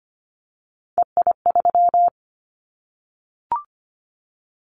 Morse code, heard as a keyed tone, about 25 wpm, 700 Hz. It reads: EI3
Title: Callsign Prefixes for Top 50 DXCC - 25wpm